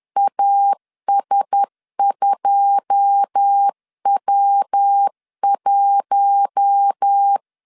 Morse code, heard as a keyed tone, about 10 wpm, 800 Hz. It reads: AS2W1
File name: CW / DET / 1 9 1500C S A 10wpm 4511